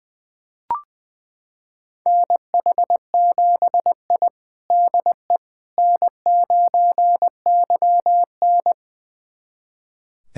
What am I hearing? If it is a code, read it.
Morse code, heard as a keyed tone, about 20 wpm, 700 Hz. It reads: NH7I DE N9YN
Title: USA Callsign DE Callsign - 4 Characters - 20wpm